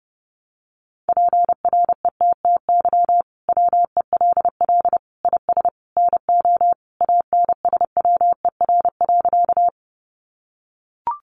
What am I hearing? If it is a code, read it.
Morse code, heard as a keyed tone, about 30 wpm, 700 Hz. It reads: PRETTY WELL IS NO ANSWER.
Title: Sentences from Top 1000 Words 30wpm